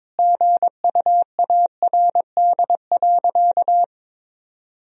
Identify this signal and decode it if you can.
Morse code, heard as a keyed tone, about 22 wpm, 700 Hz. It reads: GUARD.
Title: Sentences from Top 1000 Words - Review Part 4 - 22wpm